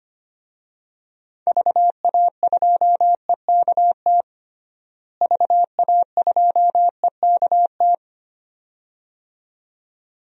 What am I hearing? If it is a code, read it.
Morse code, heard as a keyed tone, about 25 wpm, 700 Hz. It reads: VA2EKT VA2EKT
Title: DXCC-100 Call Signs - 6 Characters - Repeated 2x - 25wpm